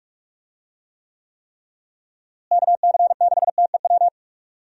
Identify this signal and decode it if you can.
Morse code, heard as a keyed tone, about 46 wpm, 700 Hz. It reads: KC6TEW
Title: USA Call Signs - 6 Characters - Repeated 3x - 45wpm